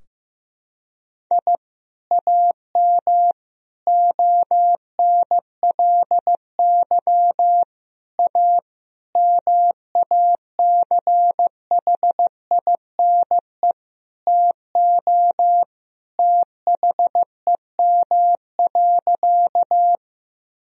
Morse code, heard as a keyed tone, about 15 wpm, 700 Hz. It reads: I AM ONLY A MACHINE TO THEM.
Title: Sentences from Top 1000 Words - Review Part 2 - 15wpm